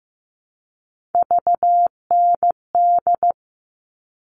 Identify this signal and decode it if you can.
Morse code, heard as a keyed tone, about 15 wpm, 700 Hz. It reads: VND
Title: All Cut Numbers - 3-Digit 15wpm